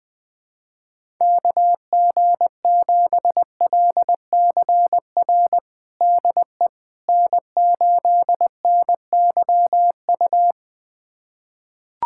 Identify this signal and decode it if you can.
Morse code, heard as a keyed tone, about 20 wpm, 700 Hz. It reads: KG7LCR DE N8NYU
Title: USA Callsign DE Callsign 20wpm